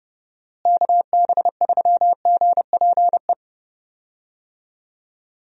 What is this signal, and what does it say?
Morse code, heard as a keyed tone, about 31 wpm, 700 Hz. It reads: KB3GPE
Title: USA Call Signs - 6 Characters - Repeated 2x - 30wpm